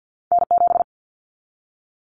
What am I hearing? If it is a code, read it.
Morse code, heard as a keyed tone, about 50 wpm, 700 Hz. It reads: N7
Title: Callsign Prefixes for Top 25 DXCC - 50wpm